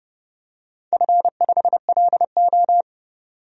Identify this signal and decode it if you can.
Morse code, heard as a keyed tone, about 30 wpm, 700 Hz. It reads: F5LO
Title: DXCC-100 Call Signs - 4 Characters - 30wpm